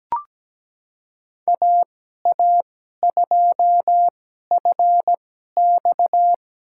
Morse code, heard as a keyed tone, about 17 wpm, spaced as for 12 wpm, 700 Hz. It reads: AA2FX
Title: USA Call Signs - 5 Characters - 2x Character Spacing - 17wpm